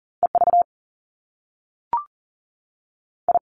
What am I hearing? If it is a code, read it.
Morse code, heard as a keyed tone, about 38 wpm, 700 Hz. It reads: E V I
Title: Top Two Letter Combinations in Words 40wpm